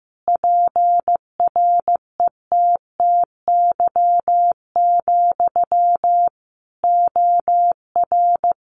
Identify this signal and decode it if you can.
Morse code, heard as a keyed tone, about 15 wpm, 700 Hz. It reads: PRETTY, OR
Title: Sentences from Top 1000 Words 15wpm